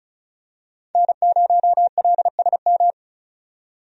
Morse code, heard as a keyed tone, about 34 wpm, 700 Hz. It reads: N0LSM
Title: USA Call Signs - 5 Characters - 35wpm